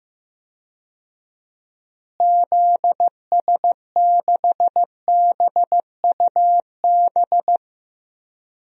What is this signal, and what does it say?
Morse code, heard as a keyed tone, about 15 wpm, 700 Hz. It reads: ZS6BUB
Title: DXCC-100 Call Signs - 6 Characters - Repeated 2x - 15wpm